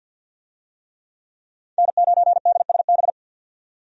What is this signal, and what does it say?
Morse code, heard as a keyed tone, about 52 wpm, 700 Hz. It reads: N9DSB
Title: USA Call Signs - 5 Characters - 50wpm